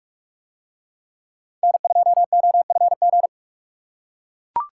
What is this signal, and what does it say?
Morse code, heard as a keyed tone, about 44 wpm, 700 Hz. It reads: N2OFG